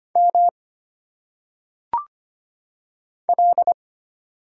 Morse code, heard as a keyed tone, about 25 wpm, 700 Hz. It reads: M L
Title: Single Letters - Phonetic Pronunciation 25wpm